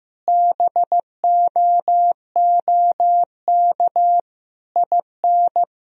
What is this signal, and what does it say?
Morse code, heard as a keyed tone, about 15 wpm, 700 Hz. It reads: BOOK IN
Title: Sentences from Top 1000 Words - Review Part 3 - 15wpm